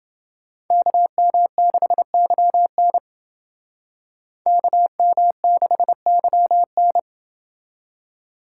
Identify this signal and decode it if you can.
Morse code, heard as a keyed tone, about 28 wpm, 700 Hz. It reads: KM6YN KM6YN